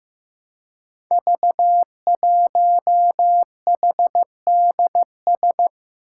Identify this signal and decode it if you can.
Morse code, heard as a keyed tone, about 15 wpm, 700 Hz. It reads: V1HDS
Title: Five Challenging Letter-Number Combinations 15wpm